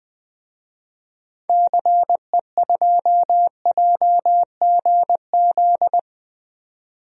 Morse code, heard as a keyed tone, about 20 wpm, 700 Hz. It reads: CE2JGZ
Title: DXCC-100 Call Signs - 6 Characters - 20wpm